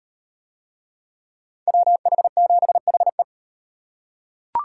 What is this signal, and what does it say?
Morse code, heard as a keyed tone, about 36 wpm, 700 Hz. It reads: WH7HE